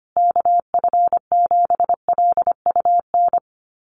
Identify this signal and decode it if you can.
Morse code, heard as a keyed tone, about 25 wpm, 700 Hz. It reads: KF7LUN
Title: USA Call Signs Repeated 3x 25wpm